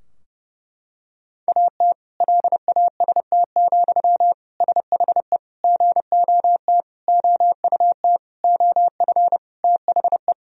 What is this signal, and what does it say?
Morse code, heard as a keyed tone, about 30 wpm, 700 Hz. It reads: AT LAST, SHE GOT OUT OF THE
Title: Sentences from Top 1000 Words - Review Part 2 - 30wpm